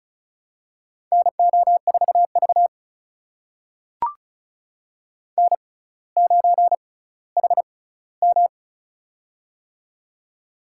Morse code, heard as a keyed tone, about 36 wpm, spaced as for 18 wpm, 700 Hz. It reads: NO4V N9HM
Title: USA Call Signs - 4 Characters - 6x Character Spacing - 35wpm